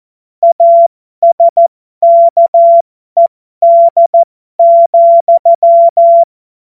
Morse code, heard as a keyed tone, about 14 wpm, spaced as for 12 wpm, 680 Hz. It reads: ASKED,
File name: Morsekurs / Witze